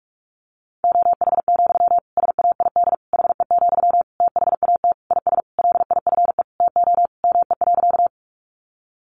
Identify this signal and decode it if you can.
Morse code, heard as a keyed tone, about 44 wpm, 700 Hz. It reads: OH, SAID HE, THAT IS LIFE TO ME.
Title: Sentences from Top 500 Words 45wpm